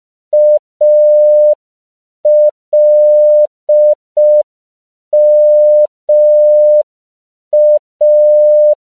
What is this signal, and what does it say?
Morse code, heard as a keyed tone, about 5 wpm, 600 Hz. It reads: ALMA
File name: CW / 5PPM